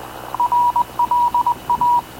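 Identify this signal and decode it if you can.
Morse code, heard as a keyed tone, about 20 wpm, 960 Hz. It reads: RLA
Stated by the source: Havana Radio / CLA sending Spanish press (PX) in Morse, recorded by Radio Officer Paul Dumesnil onboard CNAV QUEST/CZDO.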